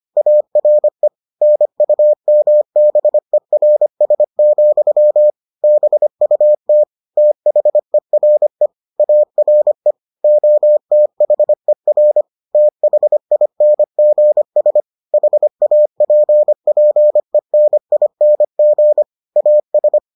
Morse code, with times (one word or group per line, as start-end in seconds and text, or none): 0.2-1.1 ARE
1.4-5.3 NUMBERS,
5.6-6.8 BUT
7.2-8.7 THERE
9.0-9.9 ARE
10.2-12.2 OTHER
12.6-14.8 THINGS
15.1-19.0 HAPPENING
19.4-20.0 AS